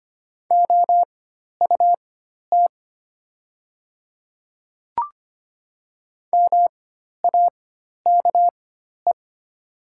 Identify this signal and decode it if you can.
Morse code, heard as a keyed tone, about 25 wpm, spaced as for 12 wpm, 700 Hz. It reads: OUT MAKE
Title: 4x Character Spacing - Top 100 Words - 25wpm